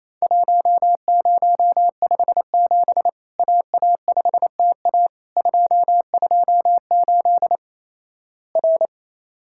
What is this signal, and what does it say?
Morse code, heard as a keyed tone, about 28 wpm, 700 Hz. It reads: 1057 AA5TA 228 R
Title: Worked All Europe DX Contest - QTC Exchange - 28wpm